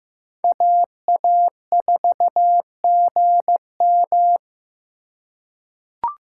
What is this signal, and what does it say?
Morse code, heard as a keyed tone, about 15 wpm, 700 Hz. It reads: AA4GM
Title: USA Call Signs - 3x Character Spacing - 15wpm